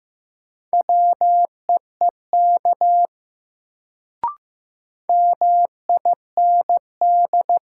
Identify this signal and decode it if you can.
Morse code, heard as a keyed tone, about 15 wpm, 700 Hz. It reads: WEEK MIND
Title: Sentences from Top 500 Words - New Word Drill - 15wpm